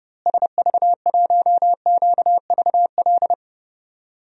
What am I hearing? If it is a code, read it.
Morse code, heard as a keyed tone, about 30 wpm, 700 Hz. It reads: SV1QVL
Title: DXCC-100 Call Signs - 6 Characters - Repeated 2x - 30wpm